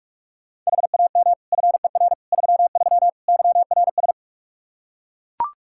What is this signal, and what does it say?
Morse code, heard as a keyed tone, about 41 wpm, 700 Hz. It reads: HAM FER 33 YRS
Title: Practice QSO - Exchange 3 - More Info - 45wpm